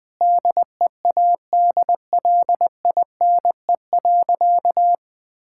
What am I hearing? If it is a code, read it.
Morse code, heard as a keyed tone, about 20 wpm, 700 Hz. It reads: DEADLINE.
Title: Sentences with 5-Word Phrases - 20wpm